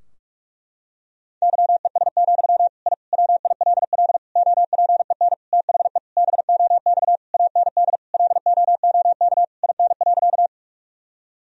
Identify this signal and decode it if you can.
Morse code, heard as a keyed tone, about 41 wpm, 700 Hz. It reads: YES, I WILL OPEN THE BOX AND LOOK IN.
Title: Sentences from Top 1000 Words - Review Part 2 - 45wpm